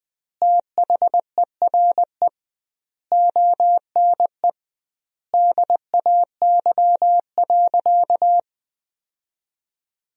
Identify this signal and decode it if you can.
Morse code, heard as a keyed tone, about 20 wpm, 700 Hz. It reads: THERE ONE DAY.
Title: 2x Word-Spacing - Sentences from Top 300 Words - 20wpm